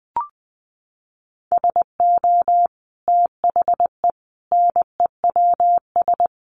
Morse code, heard as a keyed tone, about 20 wpm, 700 Hz. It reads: SO THE NEWS
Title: Sentences from Top 1200 Words 20wpm